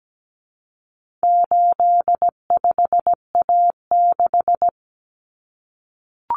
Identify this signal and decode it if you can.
Morse code, heard as a keyed tone, about 17 wpm, 700 Hz. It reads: 85A6
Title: Common Cut Numbers - 4-Digit 17wpm